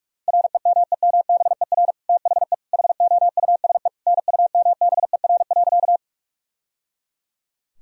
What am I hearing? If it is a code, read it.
Morse code, heard as a keyed tone, about 47 wpm, 700 Hz. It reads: REMEMBER THE HOUSE NUMBER.
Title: Sentences from Top 1000 Words - Review Part 1 - 45wpm